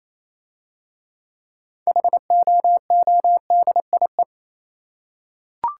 Morse code, heard as a keyed tone, about 27 wpm, 700 Hz. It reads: HOODIE
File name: Easy Words - Speed-Racing with Hints - 22wpm